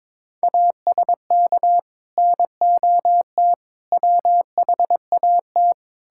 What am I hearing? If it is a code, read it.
Morse code, heard as a keyed tone, about 21 wpm, 700 Hz. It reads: ASK NOT WHAT
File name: Sets of 9 Words - 22wpm